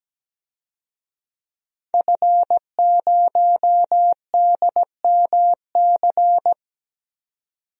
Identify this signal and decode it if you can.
Morse code, heard as a keyed tone, about 17 wpm, 700 Hz. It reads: F0DMC
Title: DXCC-100 Call Signs - 5 Characters - 17wpm